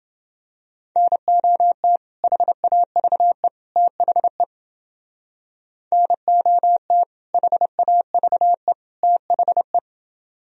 Morse code, heard as a keyed tone, about 31 wpm, 700 Hz. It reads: NOT HAVE THE NOT HAVE THE